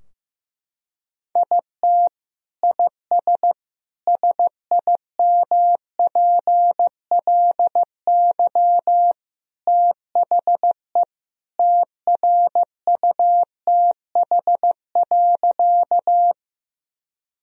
Morse code, heard as a keyed tone, about 15 wpm, 700 Hz. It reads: IT IS SIMPLY THE TRUTH.
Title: Sentences from Top 1000 Words - Review Part 4 - 15wpm